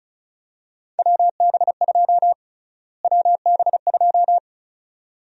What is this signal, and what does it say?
Morse code, heard as a keyed tone, about 34 wpm, 700 Hz. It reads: WB2 WB2
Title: USA Call Sign Prefixes - Repeated 2x - 35wpm